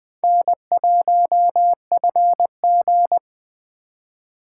Morse code, heard as a keyed tone, about 20 wpm, 700 Hz. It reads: N1FG
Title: USA Call Signs - 4 Characters - 2x Character Spacing - 20wpm